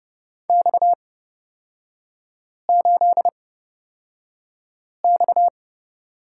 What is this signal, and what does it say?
Morse code, heard as a keyed tone, about 31 wpm, spaced as for 5 wpm, 700 Hz. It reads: X8X
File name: Lesson 37e - Letter x - Three Characters 30wpm